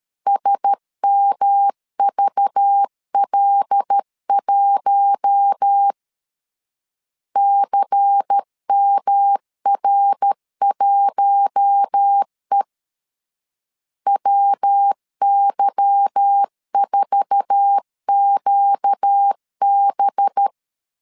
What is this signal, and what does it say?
Morse code, heard as a keyed tone, about 13 wpm, 800 Hz. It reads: SMVL1 CMR1E WY4QB
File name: LEZIONI / 8 YJ45 / 8 YJ4 12wpm 2151